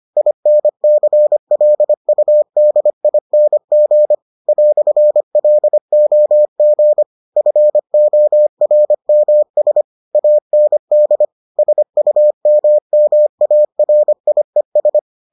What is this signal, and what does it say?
Morse code, heard as a keyed tone, about 25 wpm, 600 Hz. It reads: INCLUDING "LOG FORMS AND SUMMARIES